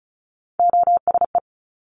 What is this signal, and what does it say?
Morse code, heard as a keyed tone, about 36 wpm, 700 Hz. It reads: OSE
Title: Top Three Letter Combinations in Words 35wpm